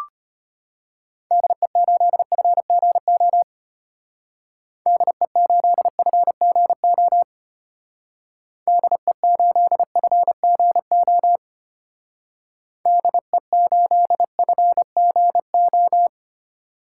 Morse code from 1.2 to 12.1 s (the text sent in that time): DE8FGO DE8FGO DE8FGO